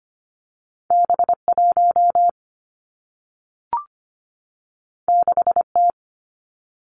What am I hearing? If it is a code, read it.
Morse code, heard as a keyed tone, about 25 wpm, 700 Hz. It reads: B1 6T
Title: Two Challenging Letter-Number Combinations 25wpm